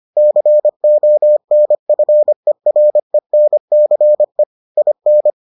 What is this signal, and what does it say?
Morse code, heard as a keyed tone, about 25 wpm, 600 Hz. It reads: CONFERENCE IN